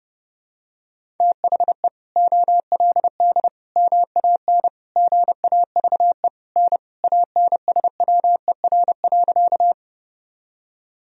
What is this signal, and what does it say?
Morse code, heard as a keyed tone, about 32 wpm, 700 Hz. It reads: THE OLD MAN GAVE N ANSWER.
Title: Sentences from Top 500 Words and Abbreviations 30wpm